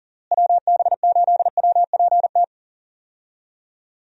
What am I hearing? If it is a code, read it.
Morse code, heard as a keyed tone, about 40 wpm, 700 Hz. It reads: WB8WPT
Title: USA Call Signs - 6 Characters - Repeated 3x - 40wpm